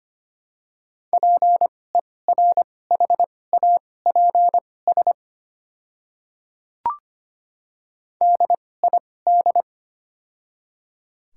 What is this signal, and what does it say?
Morse code, heard as a keyed tone, about 25 wpm, spaced as for 18 wpm, 700 Hz. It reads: PERHAPS DID
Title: 2x Character Spacing - Top 500 Words - 25wpm